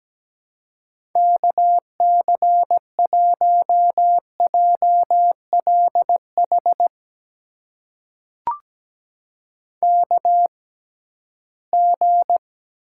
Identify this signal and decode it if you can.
Morse code, heard as a keyed tone, about 17 wpm, spaced as for 13 wpm, 700 Hz. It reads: KC1JLH KG
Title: USA Call Signs - 6 Characters - 6x Character Spacing - 17wpm